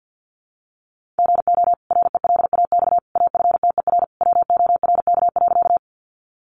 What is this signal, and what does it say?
Morse code, heard as a keyed tone, about 50 wpm, 700 Hz. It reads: GO RELAX AFTER WORK.
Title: Sentences from Top 1400 Words 50wpm